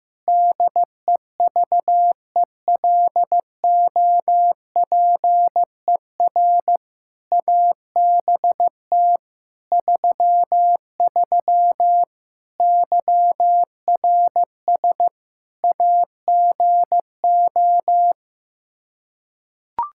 Morse code, heard as a keyed tone, about 15 wpm, 700 Hz. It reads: DEVELOPER ABT 33 YRS AGO
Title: K1USN SST QSO Practice 15wpm